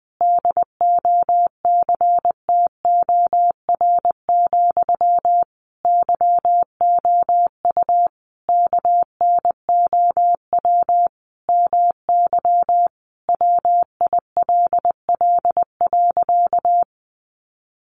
Morse code, heard as a keyed tone, about 19 wpm, 700 Hz. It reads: DOCTOR, YOU KNOW MY WILL.
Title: Sentences from Top 1500 Words 20wpm